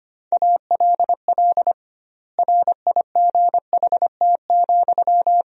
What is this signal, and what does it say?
Morse code, heard as a keyed tone, about 25 wpm, 700 Hz. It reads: ALL RIGHT,
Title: 2x Word-Spacing - Sentences from Top 200 Words - 25wpm